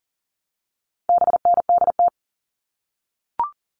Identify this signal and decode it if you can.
Morse code, heard as a keyed tone, about 40 wpm, 700 Hz. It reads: BNDT
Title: All Cut Numbers - 4-Digit 40wpm